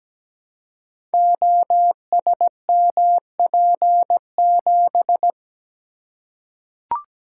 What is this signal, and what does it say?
Morse code, heard as a keyed tone, about 17 wpm, 700 Hz. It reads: OSMP7
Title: Code Groups - 5 Letter:Numbers - 17wpm